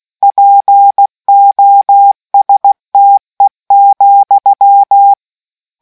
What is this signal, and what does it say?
Morse code, heard as a keyed tone, about 16 wpm, 800 Hz. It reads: POSTE,